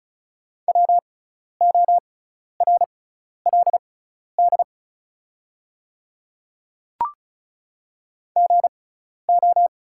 Morse code, heard as a keyed tone, about 34 wpm, spaced as for 12 wpm, 700 Hz. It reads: WORLD GO
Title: 6x Character Spacing - Top 200 Words - 35wpm